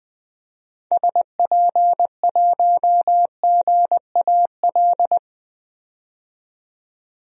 Morse code, heard as a keyed tone, about 20 wpm, 700 Hz. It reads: SP1GAL
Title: DXCC-100 Call Signs - 6 Characters - 20wpm